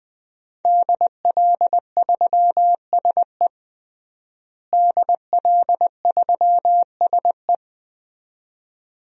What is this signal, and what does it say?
Morse code, heard as a keyed tone, about 20 wpm, 700 Hz. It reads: DL3SE DL3SE
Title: DXCC-100 Call Signs - 5 Characters - Repeated 2x - 20wpm